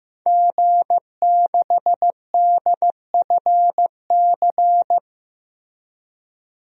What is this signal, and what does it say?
Morse code, heard as a keyed tone, about 15 wpm, 700 Hz. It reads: G6DFC
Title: DXCC-100 Call Signs - 5 Characters - Repeated 3x - 15wpm